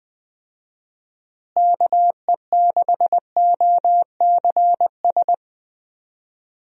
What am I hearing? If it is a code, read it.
Morse code, heard as a keyed tone, about 20 wpm, 700 Hz. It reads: KE6OCS
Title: USA Call Signs - 6 Characters - Repeated 3x - 20wpm